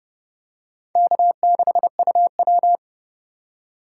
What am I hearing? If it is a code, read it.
Morse code, heard as a keyed tone, about 31 wpm, 700 Hz. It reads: K6UW